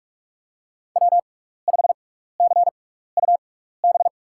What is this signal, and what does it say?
Morse code, heard as a keyed tone, about 44 wpm, spaced as for 15 wpm, 690 Hz. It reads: W5CUB
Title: USA Call Signs - 5 Characters - 6x Character Spacing - 45wpm